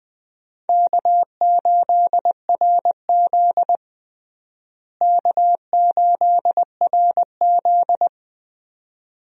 Morse code, heard as a keyed tone, about 20 wpm, 700 Hz. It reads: K8RZ K8RZ
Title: USA Call Signs - 4 Characters - Repeated 2x - 20wpm